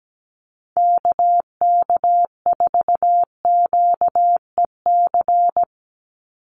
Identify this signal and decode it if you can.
Morse code, heard as a keyed tone, about 17 wpm, 700 Hz. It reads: KK4QEC
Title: USA Call Signs 17wpm